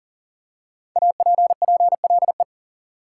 Morse code, heard as a keyed tone, about 40 wpm, 700 Hz. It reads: APPLE
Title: Five Letter Words - 40wpm